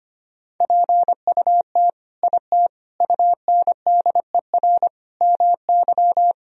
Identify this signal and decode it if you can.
Morse code, heard as a keyed tone, about 25 wpm, 700 Hz. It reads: PUT IT UNDER MY
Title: Sentences from Top 1000 Words - Review Part 1 - 25wpm